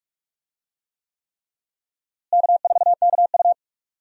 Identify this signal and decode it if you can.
Morse code, heard as a keyed tone, about 46 wpm, 700 Hz. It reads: K4KU
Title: USA Call Signs - 4 Characters - Repeated 3x - 45wpm